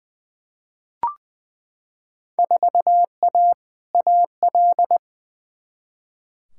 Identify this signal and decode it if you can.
Morse code, heard as a keyed tone, about 20 wpm, 700 Hz. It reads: 4A AL
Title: Field Day Exchanges - 20wpm